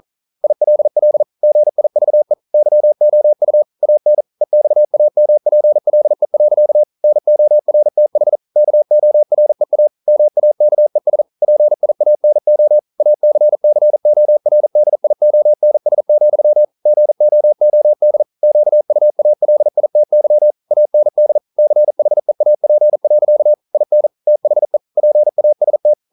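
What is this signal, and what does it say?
Morse code, heard as a keyed tone, about 38 wpm, 600 Hz. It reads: ILL GIVE YOU AN EXAMPLE. NORTH KOREA MAKES PIANO ACCORDIONS, GOOD QUALITY AND CHEAP. IN THE PAST
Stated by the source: Morse code practice at 41 wpm, from CW med Gnister.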